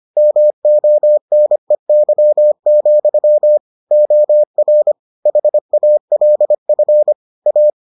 Morse code, heard as a keyed tone, about 25 wpm, 600 Hz. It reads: MONEY, OR HALF A